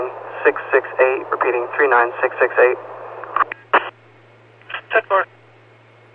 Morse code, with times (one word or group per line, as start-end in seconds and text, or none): none